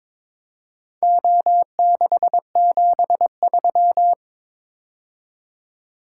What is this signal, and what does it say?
Morse code, heard as a keyed tone, about 22 wpm, 700 Hz. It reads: O673
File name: practice / 22wpm / Four Letter-Number Combinations - 22wpm